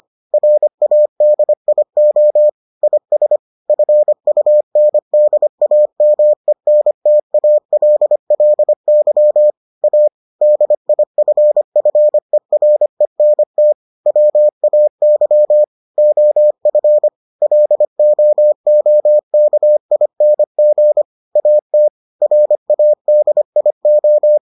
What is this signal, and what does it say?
Morse code, heard as a keyed tone, about 25 wpm, 600 Hz. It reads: RADIO IS FUNDAMENTALLY A DIFFERENT WAY OF LOOKING AT RADIO